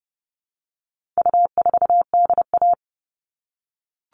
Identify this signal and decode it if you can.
Morse code, heard as a keyed tone, about 30 wpm, 700 Hz. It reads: U4DA
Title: All Cut Numbers - 4-Digit 30wpm